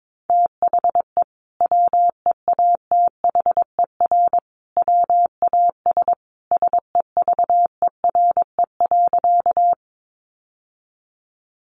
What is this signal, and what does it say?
Morse code, heard as a keyed tone, about 22 wpm, 700 Hz. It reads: THE WEATHER WAS SEVERE.
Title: Sentences from Top 1400 Words 22wpm